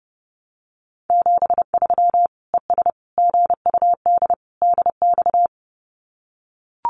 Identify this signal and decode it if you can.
Morse code, heard as a keyed tone, about 30 wpm, 700 Hz. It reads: 73 ES GUD DX
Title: Common QSO Phrases 30wpm